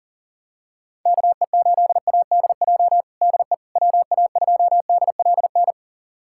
Morse code, heard as a keyed tone, about 37 wpm, 700 Hz. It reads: KE8ADJ DE WA2DLN
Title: USA Callsign DE Callsign - 6 Characters - 40wpm